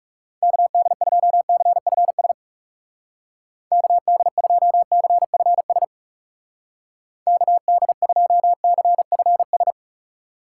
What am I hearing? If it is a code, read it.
Morse code, heard as a keyed tone, about 37 wpm, 700 Hz. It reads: KD2CFS KD2CFS KD2CFS
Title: Speed Racing - USA 2x3 Call Signs 30wpm